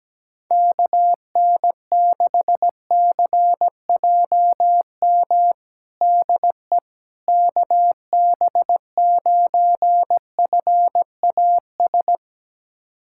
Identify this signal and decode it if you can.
Morse code, heard as a keyed tone, about 17 wpm, 700 Hz. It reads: KN6CJM DE KB9FAS
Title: USA Callsign DE Callsign 17wpm